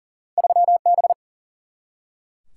Morse code, heard as a keyed tone, about 40 wpm, 700 Hz. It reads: 3B
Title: Single-Letter Single-Number 40wpm